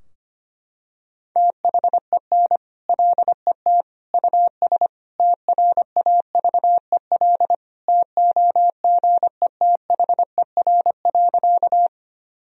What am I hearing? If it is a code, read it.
Morse code, heard as a keyed tone, about 25 wpm, 700 Hz. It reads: THEN LET US TRAVEL TOGETHER.
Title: Sentences from Top 1000 Words - Review Part 2 - 25wpm